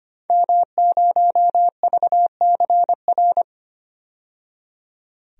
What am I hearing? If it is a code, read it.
Morse code, heard as a keyed tone, about 25 wpm, 700 Hz. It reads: M0VCR
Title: DXCC-100 Call Signs - 5 Characters - 25wpm